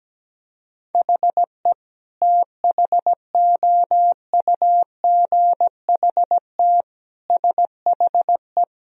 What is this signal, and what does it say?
Morse code, heard as a keyed tone, about 17 wpm, 700 Hz. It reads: HE THOUGHT SHE
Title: Sentences from Top 1000 Words - Review Part 1 - 17wpm